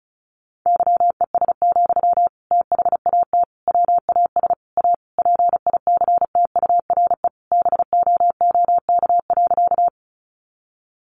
Morse code, heard as a keyed tone, about 34 wpm, 700 Hz. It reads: YES, THAT WAS A PICTURE BOOK.
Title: Sentences from Top 500 Words 35wpm